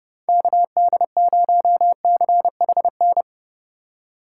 Morse code, heard as a keyed tone, about 30 wpm, 700 Hz. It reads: KD0CHN